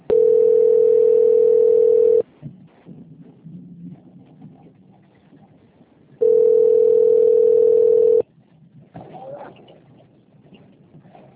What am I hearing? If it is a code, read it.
Ringback tone (North America), 440 + 480 Hz.
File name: rg-1600-unknown- - - .431